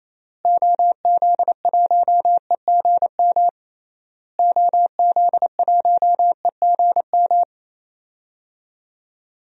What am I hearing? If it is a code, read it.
Morse code, heard as a keyed tone, about 28 wpm, 700 Hz. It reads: OZ1EGM OZ1EGM
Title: DXCC-100 Call Signs - 6 Characters - Repeated 3x - 28wpm